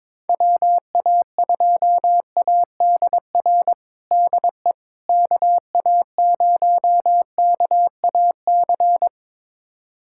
Morse code, heard as a keyed tone, about 22 wpm, 700 Hz. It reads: WA2ADR DE KA0KAC
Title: USA Callsign DE Callsign - 6 Characters - 22wpm